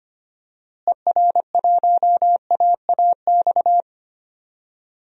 Morse code, heard as a keyed tone, about 25 wpm, 700 Hz. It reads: ER1AAX
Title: DXCC-100 Call Signs - 6 Characters - Repeated 2x - 25wpm